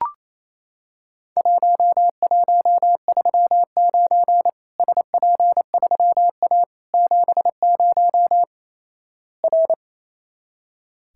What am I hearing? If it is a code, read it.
Morse code, heard as a keyed tone, about 28 wpm, 700 Hz. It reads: 1139 SP3A 70 R
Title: Worked All Europe DX Contest - QTC Exchange - 28wpm